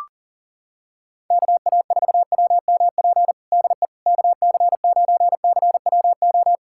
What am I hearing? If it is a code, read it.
Morse code, heard as a keyed tone, about 38 wpm, 700 Hz. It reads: KA4WMP DE KC9CWO
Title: USA Callsign DE Callsign - 6 Characters - 40wpm